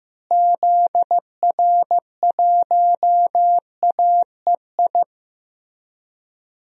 Morse code, heard as a keyed tone, about 15 wpm, 700 Hz. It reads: ZR1AEI
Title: DXCC-100 Call Signs - 6 Characters - Repeated 3x - 15wpm